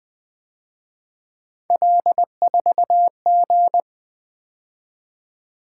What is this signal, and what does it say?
Morse code, heard as a keyed tone, about 24 wpm, 700 Hz. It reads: L4G
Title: Speed-Racing - Three Letter-Number Combinations - 20wpm